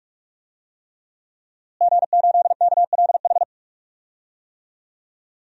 Morse code, heard as a keyed tone, about 42 wpm, 700 Hz. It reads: G8KLH
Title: DXCC-100 Call Signs - 5 Characters - Repeated 3x - 45wpm